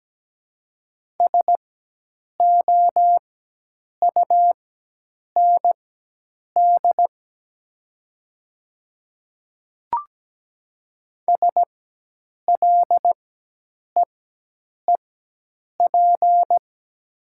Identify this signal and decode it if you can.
Morse code, heard as a keyed tone, about 17 wpm, spaced as for 8 wpm, 700 Hz. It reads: SOUND SLEEP
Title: 4x Character Spacing - Top 500 Words - 17wpm